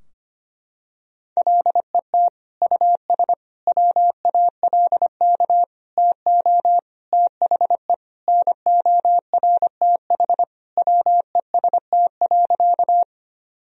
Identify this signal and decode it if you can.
Morse code, heard as a keyed tone, about 25 wpm, 700 Hz. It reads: LET US WALK TO THE NORTH WEST.
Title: Sentences from Top 1000 Words - Review Part 2 - 25wpm